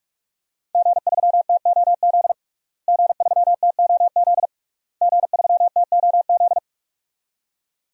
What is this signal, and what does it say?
Morse code, heard as a keyed tone, about 41 wpm, 700 Hz. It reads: G3TOZ G3TOZ G3TOZ
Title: DXCC-100 Call Signs - 5 Characters - Repeated 3x - 45wpm